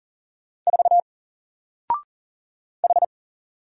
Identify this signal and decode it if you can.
Morse code, heard as a keyed tone, about 40 wpm, 700 Hz. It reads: 4 H
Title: Single Letter-Number 40wpm